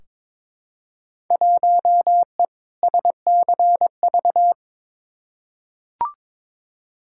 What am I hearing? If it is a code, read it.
Morse code, heard as a keyed tone, about 22 wpm, 700 Hz. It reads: 1E SCV
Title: Field Day Exchanges - 22wpm